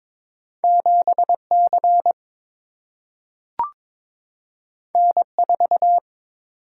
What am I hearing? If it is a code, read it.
Morse code, heard as a keyed tone, about 22 wpm, 700 Hz. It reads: 7C N4
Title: Single-Letter Single-Number 22wpm